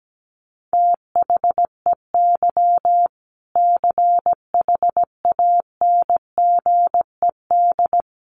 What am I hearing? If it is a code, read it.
Morse code, heard as a keyed tone, about 17 wpm, 700 Hz. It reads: THEY CHANGED
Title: A Sound Of Thunder 17wpm